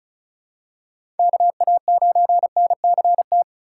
Morse code, heard as a keyed tone, about 34 wpm, 690 Hz. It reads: KA9NCT
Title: USA Call Signs - 6 Characters - 2x Character Spacing - 35wpm